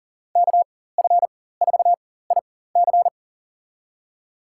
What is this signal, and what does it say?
Morse code, heard as a keyed tone, about 40 wpm, spaced as for 19 wpm, 700 Hz. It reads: KF4IC
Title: USA Call Signs - 5 Characters - 4x Character Spacing - 40wpm